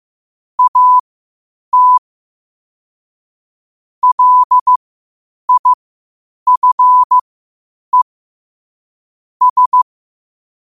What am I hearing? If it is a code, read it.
Morse code, heard as a keyed tone, about 15 wpm, spaced as for 8 wpm, 1000 Hz. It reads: AT LIFE S
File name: qotdfiles / 8wpm